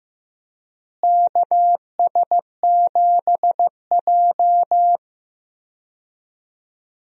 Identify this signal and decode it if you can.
Morse code, heard as a keyed tone, about 15 wpm, 700 Hz. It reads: KS7J
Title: USA Call Signs - 4 Characters - Repeated 2x - 15wpm